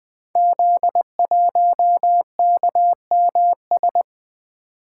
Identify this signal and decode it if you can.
Morse code, heard as a keyed tone, about 20 wpm, 700 Hz. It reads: Z1KMS
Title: Code Groups - 5 Letter:Numbers - 20wpm